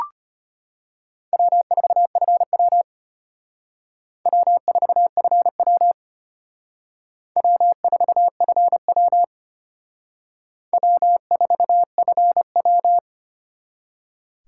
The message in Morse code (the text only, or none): W4FW W4FW W4FW W4FW